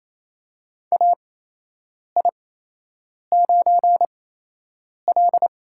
Morse code, heard as a keyed tone, about 28 wpm, spaced as for 8 wpm, 700 Hz. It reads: AI9L
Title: USA Call Signs - 8x Character Spacing - 28wpm